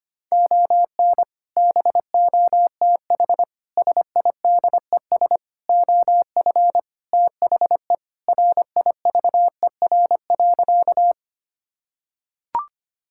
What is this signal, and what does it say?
Morse code, heard as a keyed tone, about 25 wpm, 700 Hz. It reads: ON BOTH SIDES OF THE RIVER.
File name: practice / 25wpm / Sentences with 4-Word Phrases - 25wpm